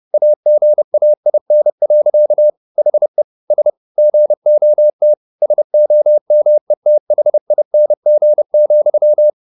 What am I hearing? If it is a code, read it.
Morse code, heard as a keyed tone, about 30 wpm, 600 Hz. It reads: AGAIN. HE S GOT SOMETHING,
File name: the old man and the sea / the old man and the sea 1 30wpm